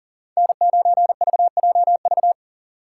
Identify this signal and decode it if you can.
Morse code, heard as a keyed tone, about 40 wpm, 700 Hz. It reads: N9VJV
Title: USA Call Signs - 5 Characters - 6x Character Spacing - 40wpm